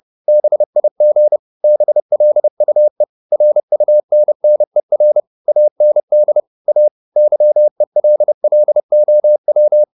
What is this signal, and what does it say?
Morse code, heard as a keyed tone, about 30 wpm, 600 Hz. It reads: BIG BLUE RUNNER AND A YELLOW
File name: the old man and the sea / the old man and the sea 1 30wpm